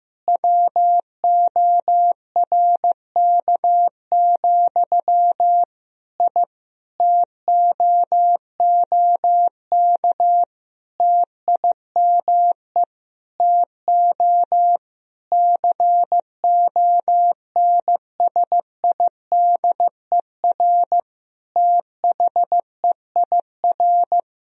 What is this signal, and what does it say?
Morse code, heard as a keyed tone, about 15 wpm, 700 Hz. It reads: WORK, I TOOK TIME TO CONSIDER THEIR
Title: Sentences from Top 1000 Words - Review Part 4 - 15wpm